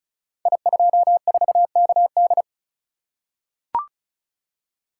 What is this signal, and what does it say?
Morse code, heard as a keyed tone, about 36 wpm, 700 Hz. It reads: I24KD
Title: Code Groups - 5 Letter:Numbers - 35wpm